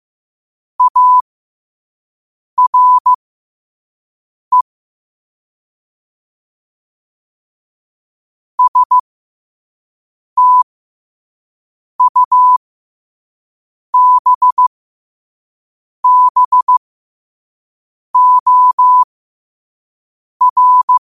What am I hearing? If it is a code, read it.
Morse code, heard as a keyed tone, about 15 wpm, spaced as for 5 wpm, 1000 Hz. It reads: ARE STUBBOR
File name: qotdfiles / 5wpm